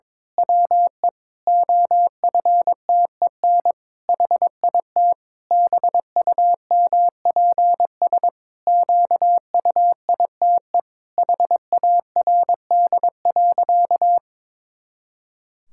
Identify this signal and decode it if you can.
Morse code, heard as a keyed tone, about 22 wpm, 700 Hz. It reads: WE OFTEN HIT BUMPS QUITE HARD.
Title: Pangram Challenge v2 - 22wpm